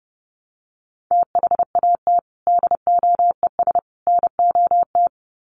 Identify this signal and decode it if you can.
Morse code, heard as a keyed tone, about 30 wpm, 700 Hz. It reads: THAT DOES NOT
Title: Sets of 3 Words 30wpm